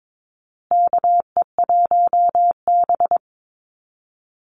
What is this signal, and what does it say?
Morse code, heard as a keyed tone, about 22 wpm, 700 Hz. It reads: KE1B
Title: Call Signs Repeated 2x - 22wpm